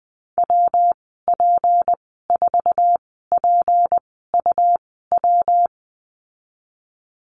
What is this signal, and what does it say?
Morse code, heard as a keyed tone, about 20 wpm, spaced as for 15 wpm, 700 Hz. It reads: WP4PUW
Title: 173 Lesson 28g - Number 9 - Call Signs 20wpm